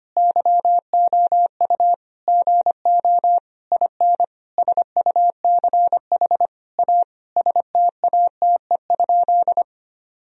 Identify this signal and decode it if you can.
Morse code, heard as a keyed tone, about 25 wpm, 700 Hz. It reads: YOU GO IN SUCH A STATE?
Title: Sentences from Top 1000 Words - Review Part 2 - 25wpm